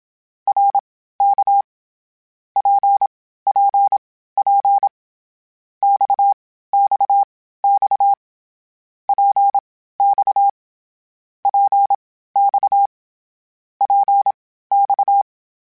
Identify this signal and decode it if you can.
Morse code, heard as a keyed tone, about 27 wpm, spaced as for 15 wpm, 800 Hz. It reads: RK PPP XXX PX PX PX
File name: Short Runs / Short Runs-mp3s / Balanced @25